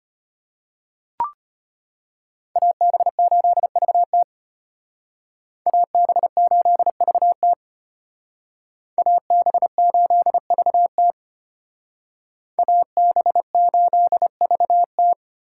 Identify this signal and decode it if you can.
Morse code, heard as a keyed tone, about 33 wpm, 700 Hz. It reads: AB8VT AB8VT AB8VT AB8VT